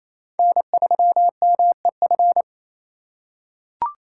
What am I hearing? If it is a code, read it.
Morse code, heard as a keyed tone, about 28 wpm, 700 Hz. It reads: N3MEF
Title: USA Call Signs - 5 Characters - 2x Character Spacing - 28wpm